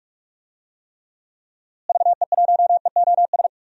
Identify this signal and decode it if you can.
Morse code, heard as a keyed tone, about 46 wpm, 700 Hz. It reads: VE1EOS
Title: DXCC-100 Call Signs - 6 Characters - Repeated 3x - 45wpm